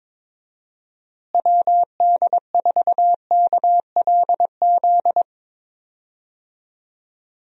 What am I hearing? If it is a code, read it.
Morse code, heard as a keyed tone, about 22 wpm, 700 Hz. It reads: WD4KLZ
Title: USA Call Signs - 6 Characters - Repeated 3x - 22wpm